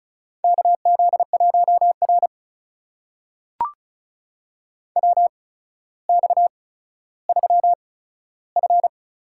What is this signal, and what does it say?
Morse code, heard as a keyed tone, about 36 wpm, spaced as for 15 wpm, 700 Hz. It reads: KZ1R WX3F
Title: USA Call Signs - 4 Characters - 8x Character Spacing - 35wpm